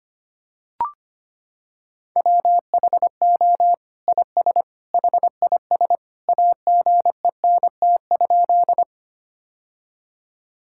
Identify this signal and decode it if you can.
Morse code, heard as a keyed tone, about 25 wpm, 700 Hz. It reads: WHO IS HIS AGENT?
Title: Sentences from Top 1600 Words 25wpm